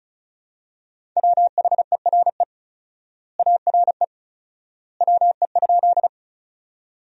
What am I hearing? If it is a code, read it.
Morse code, heard as a keyed tone, about 34 wpm, 700 Hz. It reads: WHERE ARE WE?
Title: 4x Word-Spacing - Sentences from Top 200 Words - 35wpm